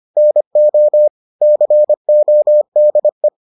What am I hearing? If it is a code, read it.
Morse code, heard as a keyed tone, about 25 wpm, 600 Hz. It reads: NO CODE